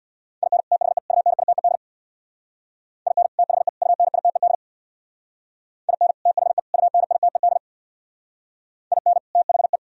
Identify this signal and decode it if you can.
Morse code, heard as a keyed tone, about 60 wpm, 700 Hz. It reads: IN THE UNITED IN THE UNITED IN THE UNITED IN THE
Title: Speed-Racing - Sets of 3 Words 50wpm